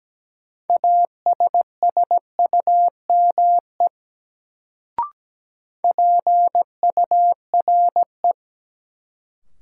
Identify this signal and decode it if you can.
Morse code, heard as a keyed tone, about 17 wpm, 700 Hz. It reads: ASSUME PURE
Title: Sentences from Top 1500 Words - New Word Drill - 17wpm